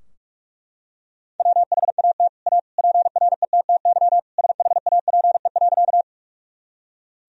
Morse code, heard as a keyed tone, about 41 wpm, 700 Hz. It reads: WHAT A PRETTY SHAPE.
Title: Sentences from Top 1000 Words - Review Part 3 - 45wpm